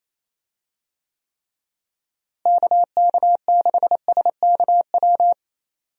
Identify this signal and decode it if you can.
Morse code, heard as a keyed tone, about 28 wpm, 700 Hz. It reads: KK6SKW
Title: USA Call Signs - 6 Characters - Repeated 3x - 28wpm